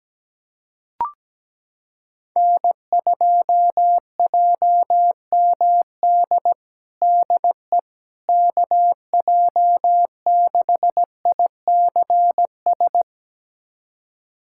Morse code, heard as a keyed tone, about 17 wpm, 700 Hz. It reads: N2JMD DE KJ6ICS
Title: USA Callsign DE Callsign 17wpm